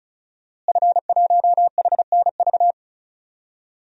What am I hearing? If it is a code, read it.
Morse code, heard as a keyed tone, about 36 wpm, 700 Hz. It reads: F1HNV
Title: DXCC-100 Call Signs - 5 Characters - Repeated 2x - 35wpm